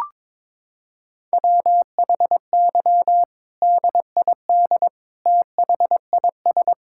Morse code, heard as a keyed tone, about 22 wpm, 700 Hz. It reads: WHY DID THIS
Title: Sentences from Top 1000 Words - Review Part 6 - 22wpm